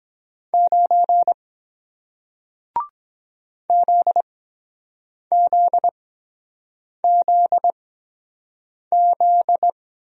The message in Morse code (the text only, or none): 9 ZZZZ